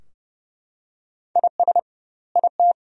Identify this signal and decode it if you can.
Morse code, heard as a keyed tone, about 30 wpm, 700 Hz. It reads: IS IT
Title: 2x Word-Spacing - Sentences from Top 500 Words - 30wpm